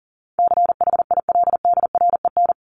Morse code, heard as a keyed tone, about 40 wpm, 700 Hz. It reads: CHILDREN
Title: Top 500 Words 40wpm